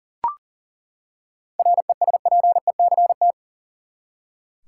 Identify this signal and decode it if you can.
Morse code, heard as a keyed tone, about 40 wpm, 700 Hz. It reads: RESPECT
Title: Seven Letter Words - 40wpm